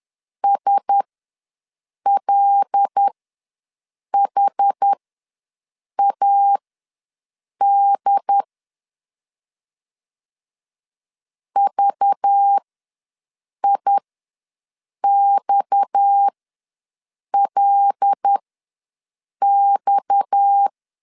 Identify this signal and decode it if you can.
Morse code, heard as a keyed tone, about 10 wpm, spaced as for 6 wpm, 800 Hz. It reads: SLHAD VIXLX
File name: LEZIONI / 6 PKX0 / 6 PKX ws 10wpm 3706